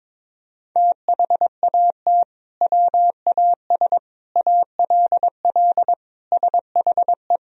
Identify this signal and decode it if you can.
Morse code, heard as a keyed tone, about 22 wpm, 700 Hz. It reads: THAT WAS ALL SHE
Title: Sentences from Top 100 Words and Abbreviations 22wpm